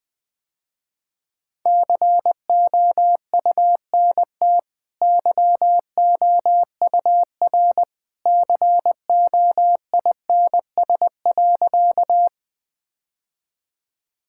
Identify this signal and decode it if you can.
Morse code, heard as a keyed tone, about 20 wpm, 700 Hz. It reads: COUNT YOUR COINS.
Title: Easy Sentences with Hints - 20wpm